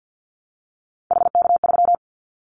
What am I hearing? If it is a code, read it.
Morse code, heard as a keyed tone, about 50 wpm, 700 Hz. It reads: HK3
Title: Callsign Prefixes for Top 100 DXCC - 50wpm